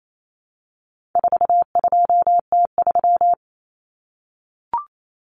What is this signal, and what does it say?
Morse code, heard as a keyed tone, about 28 wpm, 700 Hz. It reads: 42T3
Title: Common Cut Numbers - 4-Digit 28wpm